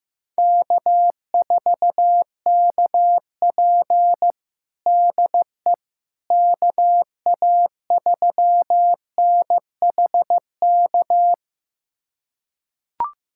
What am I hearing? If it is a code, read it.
Morse code, heard as a keyed tone, about 16 wpm, 700 Hz. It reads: K4KP DE KA3NHK
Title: USA Callsign DE Callsign 15wpm